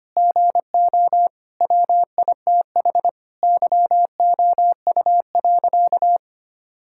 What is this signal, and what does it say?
Morse code, heard as a keyed tone, about 25 wpm, 700 Hz. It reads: GO WITH YOU.
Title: Sentences from Top 1600 Words 25wpm